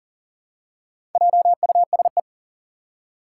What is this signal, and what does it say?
Morse code, heard as a keyed tone, about 40 wpm, 700 Hz. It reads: JUSE
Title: Four Challenging Letter-Number Combinations 40wpm